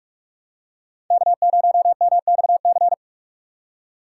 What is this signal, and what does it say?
Morse code, heard as a keyed tone, about 45 wpm, 700 Hz. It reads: K0MXC